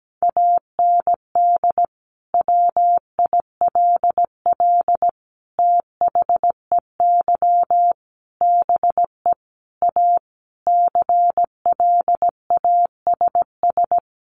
Morse code, heard as a keyed tone, about 17 wpm, 700 Hz. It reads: AND WILL THEY BE A CLASS
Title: Sentences from Top 1400 Words 17wpm